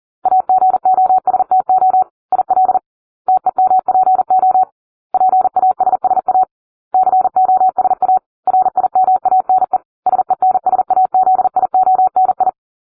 Morse code, heard as a keyed tone, about 40 wpm, 750 Hz. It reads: AGOSTO IL TEMPO PASSA COSA RIMANE SENSAZIONI